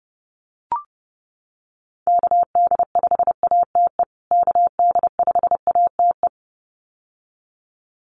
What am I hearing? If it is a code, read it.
Morse code, heard as a keyed tone, about 30 wpm, 700 Hz. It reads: KD5ATE KD5ATE
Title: USA Call Signs Repeated 2x 30wpm